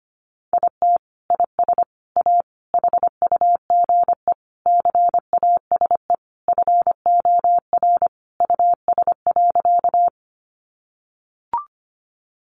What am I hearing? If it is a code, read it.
Morse code, heard as a keyed tone, about 25 wpm, 700 Hz. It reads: IT IS A HUGE CASE FOR US.
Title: Sentences from Top 1100 Words 25wpm